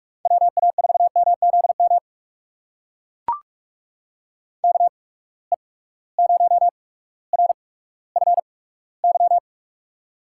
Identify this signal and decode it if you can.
Morse code, heard as a keyed tone, about 44 wpm, spaced as for 19 wpm, 700 Hz. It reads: WA4MZM KE0RFY
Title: USA Call Signs - 6 Characters - 8x Character Spacing - 45wpm